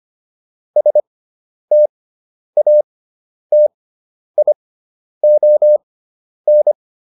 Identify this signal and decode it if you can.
Morse code, heard as a keyed tone, about 25 wpm, spaced as for 10 wpm, 600 Hz. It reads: STATION